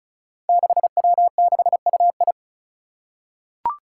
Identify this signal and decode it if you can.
Morse code, heard as a keyed tone, about 36 wpm, 700 Hz. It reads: 6W6UI
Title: Five Challenging Letter-Number Combinations 35wpm